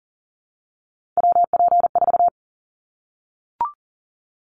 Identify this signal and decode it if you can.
Morse code, heard as a keyed tone, about 38 wpm, 700 Hz. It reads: WP4
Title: Callsign Prefixes for Top 25 DXCC - 40wpm